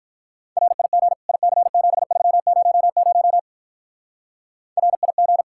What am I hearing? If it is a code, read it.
Morse code, heard as a keyed tone, about 42 wpm, 700 Hz. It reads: RIG IC7300 RIG
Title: Speed-Racing Common QSO Phrases - 35wpm